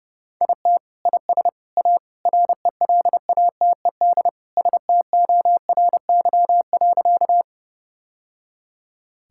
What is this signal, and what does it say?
Morse code, heard as a keyed tone, about 29 wpm, 700 Hz. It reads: IT IS A RELATED STORY.
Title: Sentences from Top 1000 Words - Review Part 5 - 30wpm